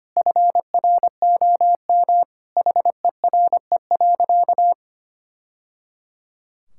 Morse code, heard as a keyed tone, about 25 wpm, 700 Hz. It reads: FROM HERE.
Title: Sentences from Top 1000 Words - Review Part 6 - 25wpm